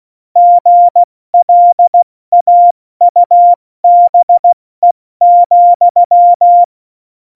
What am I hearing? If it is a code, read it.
Morse code, heard as a keyed tone, about 16 wpm, 700 Hz. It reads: GLAUBE,
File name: Morsekurs / Witze / Witz 01. September 2020, 70 F80